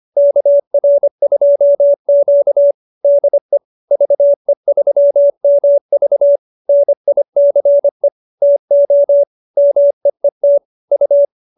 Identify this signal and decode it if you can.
Morse code, heard as a keyed tone, about 25 wpm, 590 Hz. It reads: KR2Q DE VE3MV NICE TO MEET U